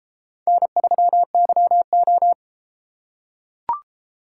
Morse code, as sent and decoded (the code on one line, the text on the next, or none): -. ...-- -.-- ---
N3YO